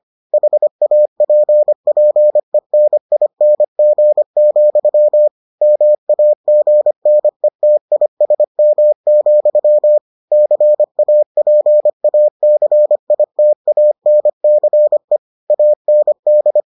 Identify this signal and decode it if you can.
Morse code, heard as a keyed tone, about 25 wpm, 600 Hz. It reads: HAPPENING, MAGNETISM, CAPACITANCE AND